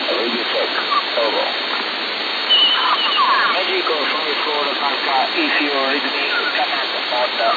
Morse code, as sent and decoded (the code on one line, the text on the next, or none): none